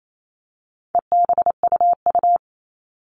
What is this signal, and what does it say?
Morse code, heard as a keyed tone, about 28 wpm, 700 Hz. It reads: EBUU
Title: All Cut Numbers - 4-Digit 28wpm